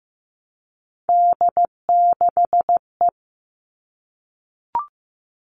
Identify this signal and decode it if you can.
Morse code, heard as a keyed tone, about 15 wpm, 700 Hz. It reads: D6E
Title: All Cut Numbers - 3-Digit 15wpm